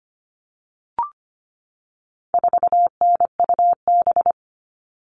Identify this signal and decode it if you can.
Morse code, heard as a keyed tone, about 25 wpm, 700 Hz. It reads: 4NUB
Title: All Cut Numbers - 4-Digit 25wpm